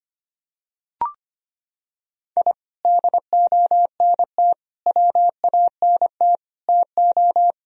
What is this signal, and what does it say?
Morse code, heard as a keyed tone, about 25 wpm, 700 Hz. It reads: I DONT WANT TO
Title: Sentences from Top 1000 Words - Review Part 4 - 25wpm